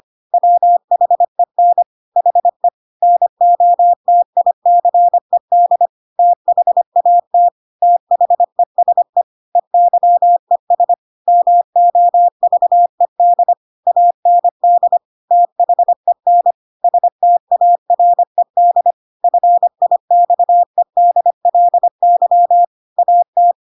Morse code, heard as a keyed tone, about 25 wpm, 700 Hz. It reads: WHEN HE NOTICED THAT THESE EYES MOVED AND THEN STARED FIXEDLY AT
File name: Pinocchio / Pinocchio ch3 25 25 ik1hge